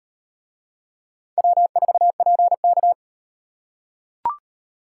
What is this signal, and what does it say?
Morse code, heard as a keyed tone, about 38 wpm, 700 Hz. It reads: W4PK